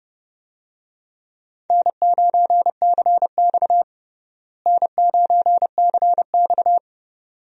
Morse code, heard as a keyed tone, about 31 wpm, 700 Hz. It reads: N9CX N9CX
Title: USA Call Signs - 4 Characters - Repeated 2x - 30wpm